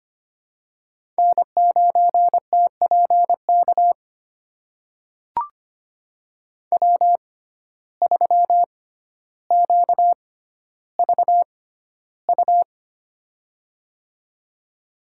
Morse code, heard as a keyed tone, about 25 wpm, spaced as for 13 wpm, 700 Hz. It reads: N9TPK W3QVU
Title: USA Call Signs - 5 Characters - 6x Character Spacing - 25wpm